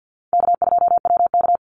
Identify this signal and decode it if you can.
Morse code, heard as a keyed tone, about 50 wpm, 700 Hz. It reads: K2WK
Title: Call Signs - 50wpm